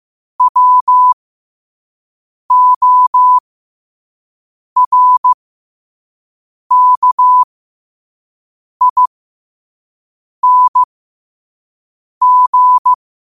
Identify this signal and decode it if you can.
Morse code, heard as a keyed tone, about 15 wpm, spaced as for 5 wpm, 1000 Hz. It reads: WORKING
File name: qotdfiles / 5wpm